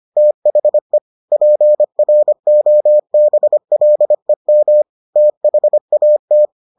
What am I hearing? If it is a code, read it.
Morse code, heard as a keyed tone, about 25 wpm, 600 Hz. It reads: THE PROBLEM THAT